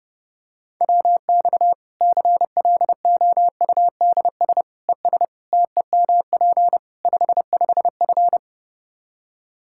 Morse code, heard as a keyed tone, about 30 wpm, 700 Hz. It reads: WX CLOUDS ES TEMP 55F
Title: K1USN SST QSO Practice 30wpm